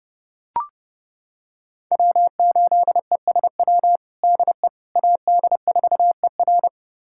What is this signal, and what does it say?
Morse code, heard as a keyed tone, about 32 wpm, 700 Hz. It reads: W8ESW DE AD4ER
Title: USA Callsign DE Callsign - 5 Characters - 30wpm